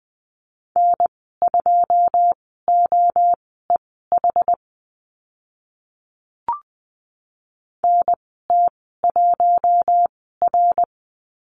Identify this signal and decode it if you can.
Morse code, heard as a keyed tone, about 20 wpm, spaced as for 15 wpm, 700 Hz. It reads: N2OEH NT1R
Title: 082 Lesson 14g - Number 2 - Call Signs 20wpm